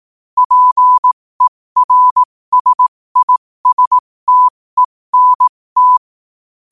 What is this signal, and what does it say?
Morse code, heard as a keyed tone, about 18 wpm, spaced as for 15 wpm, 1000 Hz. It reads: PERSISTENT
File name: qotdfiles / 15wpm